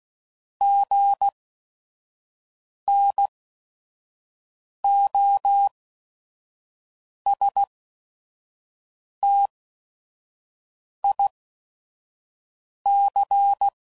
Morse code, heard as a keyed tone, about 16 wpm, spaced as for 5 wpm, 800 Hz. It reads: GNOSTIC
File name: Prefixes / By First Letter / Gs